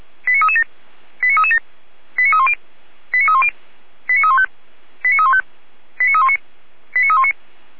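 Five-tone selective call (ZVEI), heard as quick runs of five tones, tones of about 70 ms. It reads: 89337 89337 89310 89310 89315 89315 89319 89319
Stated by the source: ZVEI five-tone call received on SDR.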